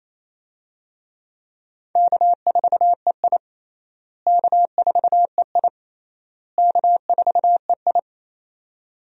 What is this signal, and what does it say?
Morse code, heard as a keyed tone, about 28 wpm, 700 Hz. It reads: K4EI K4EI K4EI
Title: USA Call Signs - 4 Characters - Repeated 3x - 28wpm